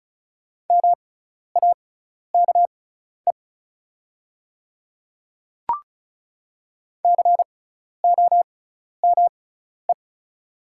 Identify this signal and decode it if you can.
Morse code, heard as a keyed tone, about 34 wpm, spaced as for 12 wpm, 700 Hz. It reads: MAKE COME
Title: 6x Character Spacing - Top 100 Words - 35wpm